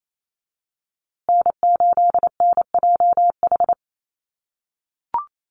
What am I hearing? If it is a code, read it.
Morse code, heard as a keyed tone, about 28 wpm, 700 Hz. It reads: N8NJH
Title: USA Call Signs Repeated 3x 28wpm